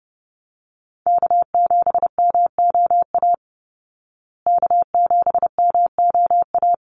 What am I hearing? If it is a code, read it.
Morse code, heard as a keyed tone, about 30 wpm, 700 Hz. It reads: K7MOA K7MOA
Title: Call Signs Repeated 3x - 30wpm